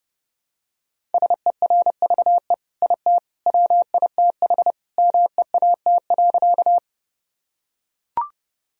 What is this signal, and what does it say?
Morse code, heard as a keyed tone, about 30 wpm, 700 Hz. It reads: SERVE IT WITH MEAT.
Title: Sentences from Top 1000 Words - Review Part 3 - 30wpm